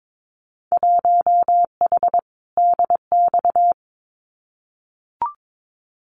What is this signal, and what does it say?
Morse code, heard as a keyed tone, about 22 wpm, 700 Hz. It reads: 1H DX
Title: Winter Field Day Exchanges 22wpm